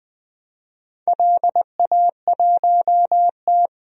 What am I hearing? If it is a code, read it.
Morse code, heard as a keyed tone, about 20 wpm, 700 Hz. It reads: LA1T
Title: DXCC-100 Call Signs - 4 Characters - 20wpm